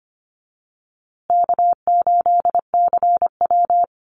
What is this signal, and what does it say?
Morse code, heard as a keyed tone, about 25 wpm, 700 Hz. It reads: K8CW
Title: USA Call Signs Repeated 3x 25wpm